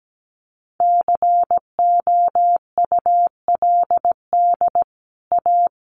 Morse code, heard as a keyed tone, about 17 wpm, 700 Hz. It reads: COULD A
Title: Sentences from Top 1200 Words 17wpm